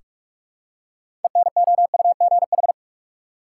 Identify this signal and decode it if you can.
Morse code, heard as a keyed tone, about 43 wpm, 700 Hz. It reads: ENOUGH
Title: Six Letter Words - 45wpm